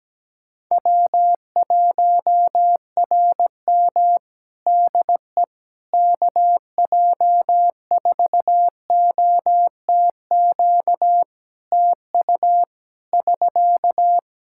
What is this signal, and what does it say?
Morse code, heard as a keyed tone, about 17 wpm, 700 Hz. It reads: W1RM DE KJ4OTQ TU <SK>
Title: Practice QSO - Exchange 4 - Ending - 17wpm